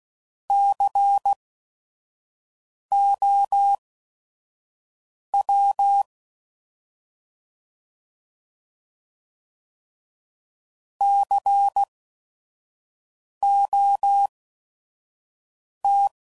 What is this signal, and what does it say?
Morse code, heard as a keyed tone, about 16 wpm, spaced as for 5 wpm, 800 Hz. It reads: COW COT
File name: Prefixes / By First Letter / Cs